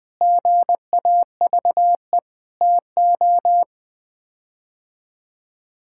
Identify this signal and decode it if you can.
Morse code, heard as a keyed tone, about 21 wpm, 700 Hz. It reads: GAVE TO
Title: Sets of 8 Words - 20wpm